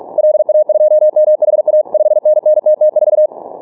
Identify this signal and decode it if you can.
Morse code, heard as a keyed tone, about 44 wpm, 610 Hz. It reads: LA2MHA 5NN TT4
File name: training / morserunner / Example